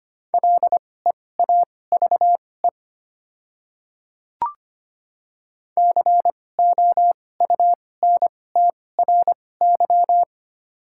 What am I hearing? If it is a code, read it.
Morse code, heard as a keyed tone, about 25 wpm, spaced as for 18 wpm, 700 Hz. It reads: LEAVE COUNTRY
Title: 2x Character Spacing - Top 500 Words - 25wpm